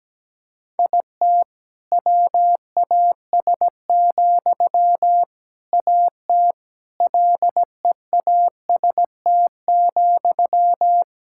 Morse code, heard as a keyed tone, about 17 wpm, 700 Hz. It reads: IT WAS, AT LEAST,
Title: Sentences from Top 1000 Words - Review Part 6 - 17wpm